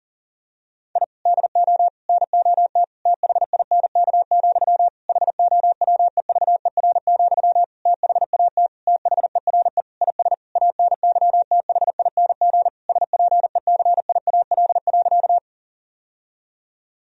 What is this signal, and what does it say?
Morse code, heard as a keyed tone, about 40 wpm, 700 Hz. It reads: I DO NOT THINK, HOWEVER, THAT THERE IS ANYTHING SPECIAL.
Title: Sentences from Top 1000 Words - Review Part 3 - 40wpm